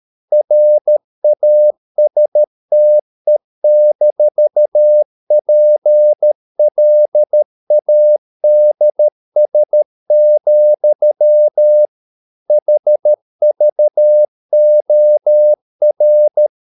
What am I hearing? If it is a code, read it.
Morse code, heard as a keyed tone, about 13 wpm, 600 Hz. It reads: RASTE-PLADS, HVOR
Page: Ferie 13 wpm